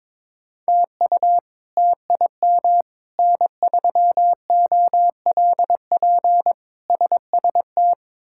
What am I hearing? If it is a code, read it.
Morse code, heard as a keyed tone, about 22 wpm, 700 Hz. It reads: TU TIM N3OLP SST
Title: K1USN SST QSO Practice - 22wpm